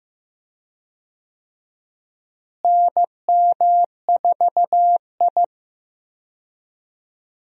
Morse code, heard as a keyed tone, about 15 wpm, 700 Hz. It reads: NM4I
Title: USA Call Signs - 4 Characters - Repeated 3x - 15wpm